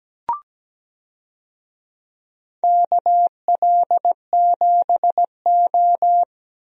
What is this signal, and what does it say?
Morse code, heard as a keyed tone, about 17 wpm, 700 Hz. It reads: KL7O
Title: DXCC-100 Call Signs - 4 Characters - Repeated 3x - 17wpm